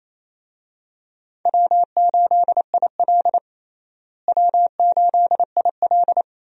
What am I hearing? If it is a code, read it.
Morse code, heard as a keyed tone, about 28 wpm, 700 Hz. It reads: W8IL W8IL
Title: USA Call Signs - 4 Characters - Repeated 2x - 28wpm